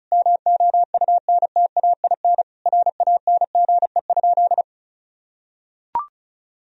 Morse code, heard as a keyed tone, about 34 wpm, 700 Hz. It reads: MOUNTAIN RANGE?
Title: Sentences from Top 1000 Words - Review Part 4 - 35wpm